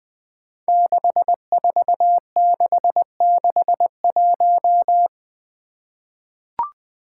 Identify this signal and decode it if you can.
Morse code, heard as a keyed tone, about 20 wpm, 700 Hz. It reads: 64661
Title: Five Challenging Letter-Number Combinations 20wpm